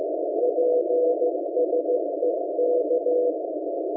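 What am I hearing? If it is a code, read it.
Morse code, heard as a keyed tone, about 15 wpm, 490 Hz, with noise. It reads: PSE K
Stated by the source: SDR receiver Radioberry, Chinese-made transceiver from AliExpress, 14.058.67 MHz CW upper sideband, posted on RU-QRP Club forum.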